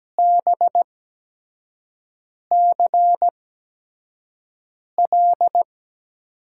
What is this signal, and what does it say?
Morse code, heard as a keyed tone, about 17 wpm, spaced as for 5 wpm, 700 Hz. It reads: BCL
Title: USA Call Signs - 6 Characters - 8x Character Spacing - 17wpm